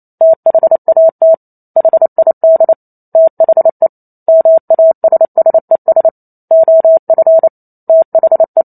Morse code, heard as a keyed tone, about 28 wpm, 650 Hz. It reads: THAT HID THE MASSES OF THE